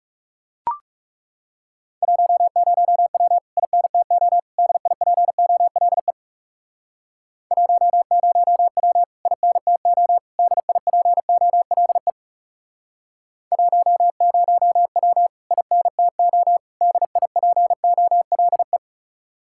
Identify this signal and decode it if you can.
Morse code, heard as a keyed tone, about 38 wpm, 700 Hz. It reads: 10W INTO DIPOLE 10W INTO DIPOLE 10W INTO DIPOLE